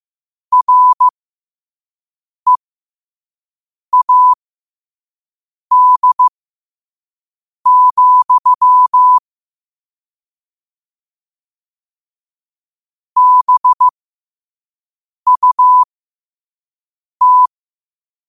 Morse code, heard as a keyed tone, about 15 wpm, spaced as for 5 wpm, 1000 Hz. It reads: READ, BUT
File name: qotdfiles / 5wpm